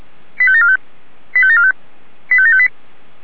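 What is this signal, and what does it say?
Five-tone selective call (ZVEI), heard as quick runs of five tones, tones of about 70 ms. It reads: 85745 85745 85758
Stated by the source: ZVEI five-tone selective call received by SDR.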